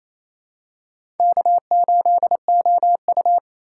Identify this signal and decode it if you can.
Morse code, heard as a keyed tone, about 27 wpm, 700 Hz. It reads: K8OU
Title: USA Call Signs - 4 Characters - 3x Character Spacing - 28wpm